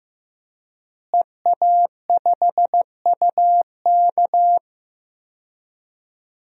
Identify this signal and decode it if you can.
Morse code, heard as a keyed tone, about 15 wpm, 700 Hz. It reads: EA5UK
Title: DXCC-100 Call Signs - 5 Characters - Repeated 3x - 15wpm